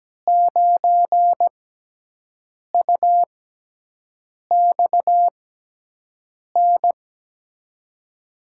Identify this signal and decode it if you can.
Morse code, heard as a keyed tone, about 17 wpm, spaced as for 6 wpm, 700 Hz. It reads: 9UXN
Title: USA Call Signs - 6x Character Spacing - 17wpm